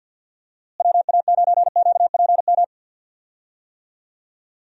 Morse code, heard as a keyed tone, about 46 wpm, 700 Hz. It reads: WA9QPM
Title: USA Call Signs - 6 Characters - Repeated 3x - 50wpm